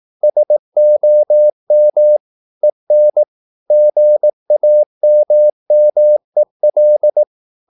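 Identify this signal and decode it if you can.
Morse code, heard as a keyed tone, about 18 wpm, 600 Hz. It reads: SOM EN GAMMEL
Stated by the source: Slow (QRS) Morse practice at 18 wpm, CW med Gnister.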